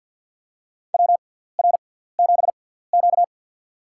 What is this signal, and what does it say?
Morse code, heard as a keyed tone, about 50 wpm, spaced as for 17 wpm, 700 Hz. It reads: WR7Q
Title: USA Call Signs - 6x Character Spacing - 50wpm